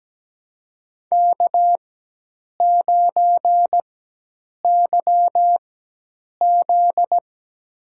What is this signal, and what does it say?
Morse code, heard as a keyed tone, about 17 wpm, spaced as for 8 wpm, 700 Hz. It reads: K9YZ
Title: USA Call Signs - 4 Characters - 4x Character Spacing - 17wpm